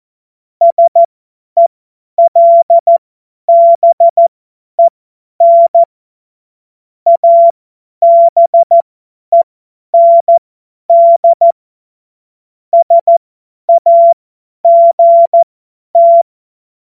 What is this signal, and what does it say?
Morse code, heard as a keyed tone, about 14 wpm, spaced as for 10 wpm, 680 Hz. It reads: SELBEN ABEND SAGT
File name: Morsekurs / Witze / Witz 12. Oktober 2021, 50 F70